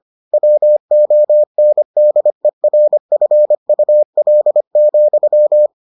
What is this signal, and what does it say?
Morse code, heard as a keyed tone, about 25 wpm, 600 Hz. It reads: WONDERFUL,